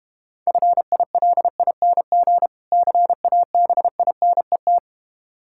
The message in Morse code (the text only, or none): FILING CABINET